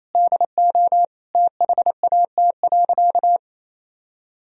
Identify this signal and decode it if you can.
Morse code, heard as a keyed tone, about 28 wpm, 700 Hz. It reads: DO THAT.
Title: Sentences from Top 1000 Words - Review Part 3 - 28wpm